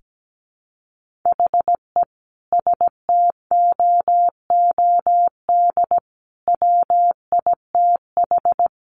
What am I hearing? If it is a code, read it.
Morse code, heard as a keyed tone, about 17 wpm, 700 Hz. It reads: HE STOOD WITH
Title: Sentences from Top 1500 Words 17wpm